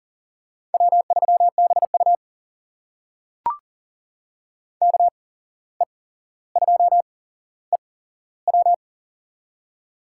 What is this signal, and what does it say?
Morse code, heard as a keyed tone, about 40 wpm, spaced as for 16 wpm, 700 Hz. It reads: W3BU KE2EW
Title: USA Call Signs - 8x Character Spacing - 40wpm